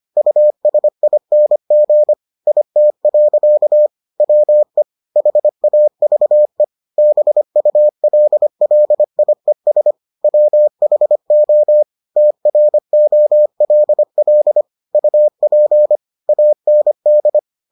Morse code, heard as a keyed tone, about 25 wpm, 600 Hz. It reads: USING IT. WE HAVE BULLIES WHO TROLL UP AND